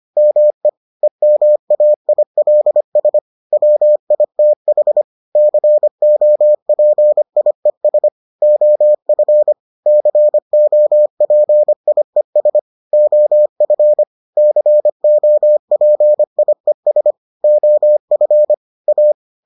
Morse code, heard as a keyed tone, about 25 wpm, 600 Hz. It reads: ME EMAILS WITH COPIES OF COPIES OF COPIES OF A